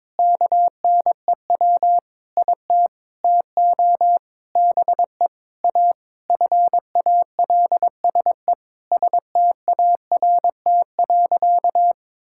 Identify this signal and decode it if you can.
Morse code, heard as a keyed tone, about 21 wpm, 700 Hz. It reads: KNEW IT TO BE A FALSE START.
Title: Sentences from Top 1600 Words 22wpm